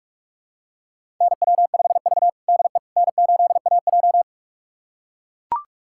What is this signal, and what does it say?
Morse code, heard as a keyed tone, about 41 wpm, 700 Hz. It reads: NW5V DE N8AJ
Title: USA Callsign DE Callsign - 4 Characters - 45wpm